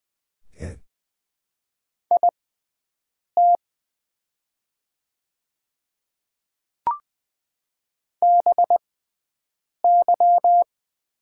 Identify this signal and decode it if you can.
Morse code, heard as a keyed tone, about 20 wpm, spaced as for 7 wpm, 700 Hz. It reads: IT BY
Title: 6x Character Spacing - Top 300 Words - 20wpm